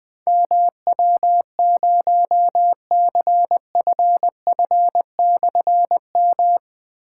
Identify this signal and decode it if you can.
Morse code, heard as a keyed tone, about 20 wpm, 700 Hz. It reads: MW0CFF/M
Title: Difficult Call Signs 20wpm